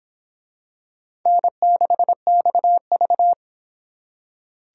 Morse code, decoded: N6XV

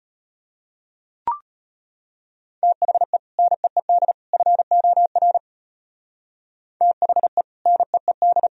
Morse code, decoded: THE NEED FOR THE NEED

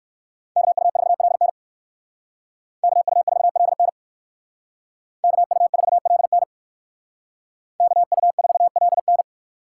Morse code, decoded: KU4LN KU4LN KU4LN KU4LN

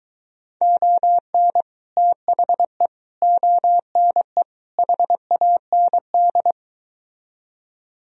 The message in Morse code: ON THE ONE HAND

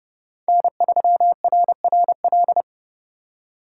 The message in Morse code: N3RRL

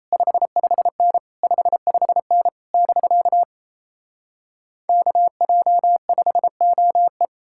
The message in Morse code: 55N 55N <BK> KJ5OE

